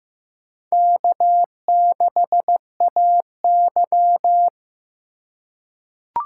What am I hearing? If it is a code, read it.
Morse code, heard as a keyed tone, about 16 wpm, 700 Hz. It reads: K6AY
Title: USA Call Signs - 4 Characters - 2x Character Spacing - 15wpm